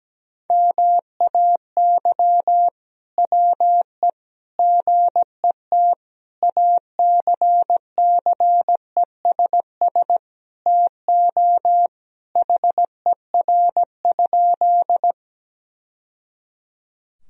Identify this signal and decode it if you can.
Morse code, heard as a keyed tone, about 17 wpm, 700 Hz. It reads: MAY WE GET ACCESS TO HER?
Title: Sentences from Top 1000 Words - Review Part 4 - 17wpm